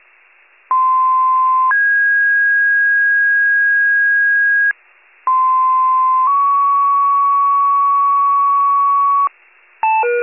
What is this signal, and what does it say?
Two-tone page, heard as a one-second tone, then a three-second tone, about 1040 then 1650 Hz, about 1040 then 1120 Hz, with noise. It is heard from 0.7 to 9.3 s.